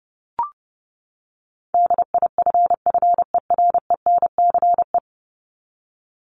Morse code, heard as a keyed tone, about 30 wpm, 700 Hz. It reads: DIFFERENCE
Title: Top 751-1000 Words 30wpm